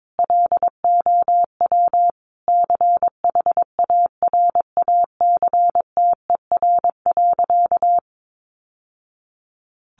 Morse code, heard as a keyed tone, about 22 wpm, 700 Hz. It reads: LOW CHARACTER.
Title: Sentences from Top 1200 Words 22wpm